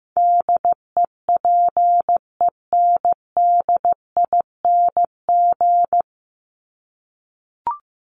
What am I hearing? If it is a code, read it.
Morse code, heard as a keyed tone, about 15 wpm, 700 Hz. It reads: DEPENDING
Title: ING Suffix - Top 100 Words with Base Words 15wpm